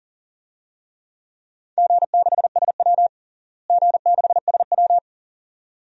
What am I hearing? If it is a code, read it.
Morse code, heard as a keyed tone, about 40 wpm, 700 Hz. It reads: G6SW G6SW
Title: DXCC-100 Call Signs - 4 Characters - Repeated 2x - 40wpm